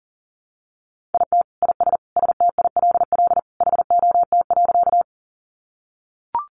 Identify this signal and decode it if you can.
Morse code, heard as a keyed tone, about 40 wpm, 700 Hz. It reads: IT IS STILL HOT.
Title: Additional Easy Phrases 40wpm